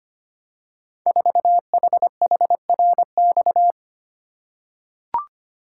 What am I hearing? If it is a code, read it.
Morse code, heard as a keyed tone, about 25 wpm, 700 Hz. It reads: 4HHRX
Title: Code Groups - 5 Letter:Numbers - 25wpm